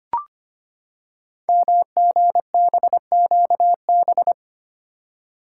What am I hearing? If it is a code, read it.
Morse code, heard as a keyed tone, about 25 wpm, 700 Hz. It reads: MGBQB
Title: Code Groups - 5 Letters - 25wpm